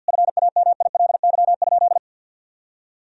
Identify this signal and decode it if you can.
Morse code, heard as a keyed tone, about 50 wpm, 700 Hz. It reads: FAMILY?
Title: Sentences from Top 1000 Words - Review Part 3 - 50wpm